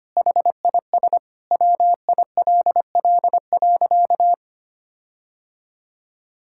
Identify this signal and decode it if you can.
Morse code, heard as a keyed tone, about 25 wpm, 700 Hz. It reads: HIS WILL.
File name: practice / 25wpm / Sentences with 9-Word Phrases - 25wpm